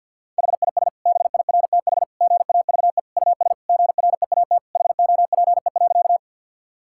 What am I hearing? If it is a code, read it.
Morse code, heard as a keyed tone, about 52 wpm, 700 Hz. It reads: HIS BIRTH GAVE US GREAT HOPE.
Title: Sentences from Top 1000 Words - Review Part 5 - 50wpm